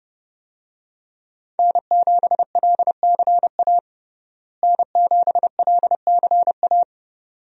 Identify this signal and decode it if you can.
Morse code, heard as a keyed tone, about 31 wpm, 700 Hz. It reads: N7LCA N7LCA
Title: USA Call Signs - 5 Characters - Repeated 2x - 30wpm